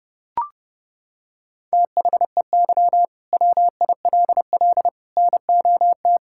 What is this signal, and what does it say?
Morse code, heard as a keyed tone, about 30 wpm, 700 Hz. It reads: THEY WILL NOT
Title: Sentences from Top 1000 Words - Review Part 5 - 30wpm